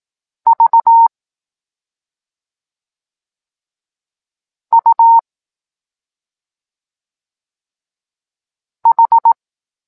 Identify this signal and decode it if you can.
Morse code, heard as a keyed tone, about 18 wpm, 920 Hz. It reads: V U H
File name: Tekst-9-rownomierny